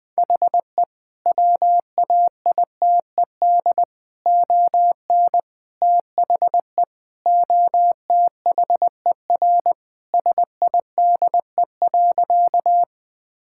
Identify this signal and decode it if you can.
Morse code, heard as a keyed tone, about 20 wpm, 700 Hz. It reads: HE WAITED ON THE OTHER SIDE.
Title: Sentences with 4-Word Phrases - 20wpm